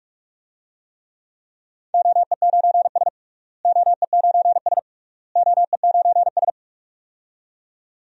Morse code, heard as a keyed tone, about 45 wpm, 700 Hz. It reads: OE9S OE9S OE9S
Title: DXCC-100 Call Signs - 4 Characters - Repeated 3x - 45wpm